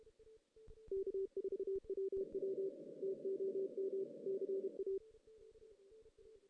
Morse code, heard as a keyed tone, about 31 wpm, 390 Hz. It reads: K4WW TOM CA
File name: 2016 JAN CW NAQP / 2016 Jan naqp k5ot - 0-2